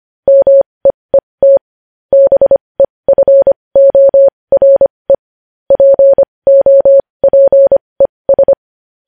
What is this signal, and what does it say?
Morse code, heard as a keyed tone, about 25 wpm, spaced as for 20 wpm, 550 Hz. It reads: MEET BEFORE POPES
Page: news headlines in morse code at 20 wpm